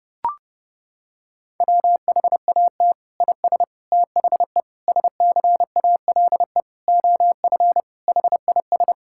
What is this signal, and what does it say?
Morse code, heard as a keyed tone, about 30 wpm, 700 Hz. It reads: WHAT IS THE SCALE OF HIS
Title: Sentences from Top 1000 Words - Review Part 5 - 30wpm